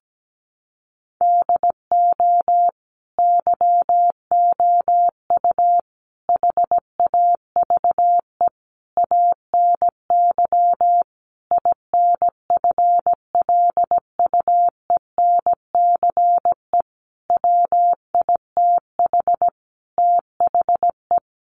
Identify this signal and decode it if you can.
Morse code, heard as a keyed tone, about 17 wpm, 700 Hz. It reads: DO YOU HAVE ANY INFLUENCE WITH THE
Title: Sentences from Top 1100 Words 17wpm